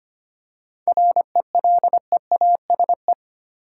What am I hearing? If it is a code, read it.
Morse code, heard as a keyed tone, about 25 wpm, 700 Hz. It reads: RELEASE
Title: Seven Letter Words - 25wpm